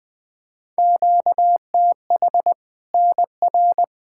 Morse code, heard as a keyed tone, about 20 wpm, 700 Hz. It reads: QTH NR
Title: Practice QSO - Exchange 1 - Beginning - 20wpm